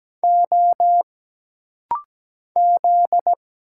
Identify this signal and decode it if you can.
Morse code, heard as a keyed tone, about 17 wpm, 700 Hz. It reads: O Z
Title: Single Letters - Rapid-Fire 17wpm